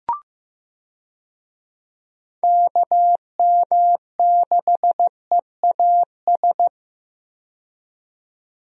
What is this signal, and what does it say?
Morse code, heard as a keyed tone, about 15 wpm, 700 Hz. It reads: KM6EAS
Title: USA Call Signs - 6 Characters - 15wpm